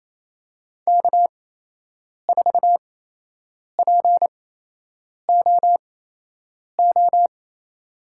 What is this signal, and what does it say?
Morse code, heard as a keyed tone, about 28 wpm, spaced as for 8 wpm, 700 Hz. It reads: K4POO
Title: USA Call Signs - 5 Characters - 8x Character Spacing - 28wpm